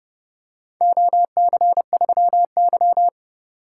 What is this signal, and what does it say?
Morse code, heard as a keyed tone, about 31 wpm, 700 Hz. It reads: OC3Y